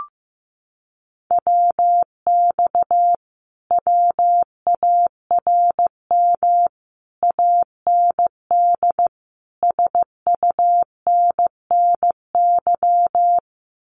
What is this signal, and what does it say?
Morse code, heard as a keyed tone, about 15 wpm, 700 Hz. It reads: WX WARM AND SUNNY
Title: Common QSO Phrases Encore 15wpm